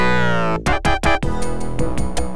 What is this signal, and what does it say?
There is no code.